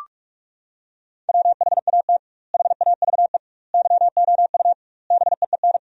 Morse code, heard as a keyed tone, about 46 wpm, 700 Hz. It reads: WHAT HAVE YOU BEEN
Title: 2x Word-Spacing - Sentences from Top 200 Words - 45wpm